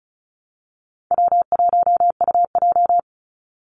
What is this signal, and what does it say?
Morse code, heard as a keyed tone, about 34 wpm, 700 Hz. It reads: W1UJ
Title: Call Signs - 35wpm